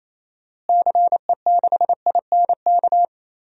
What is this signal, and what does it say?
Morse code, heard as a keyed tone, about 28 wpm, 700 Hz. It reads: CE6INK
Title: DXCC-100 Call Signs - 6 Characters - Repeated 2x - 28wpm